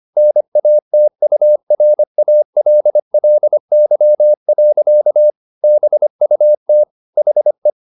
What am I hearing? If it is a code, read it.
Morse code, heard as a keyed tone, about 25 wpm, 600 Hz. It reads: NATURALLY. BUT HE
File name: the old man and the sea part 1 25wpm